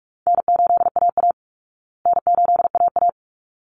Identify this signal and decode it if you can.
Morse code, heard as a keyed tone, about 44 wpm, 700 Hz. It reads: N8AA N8AA
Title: Easy Call Signs Repeated 2x - 45wpm